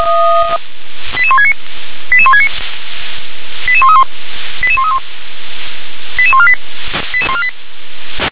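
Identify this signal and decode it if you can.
Five-tone selective call (ZVEI), heard as quick runs of five tones, tones of about 70 ms. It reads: 88169 88169 88131 88131 88148 88148
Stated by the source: ZVEI five-tone call received on an SDR.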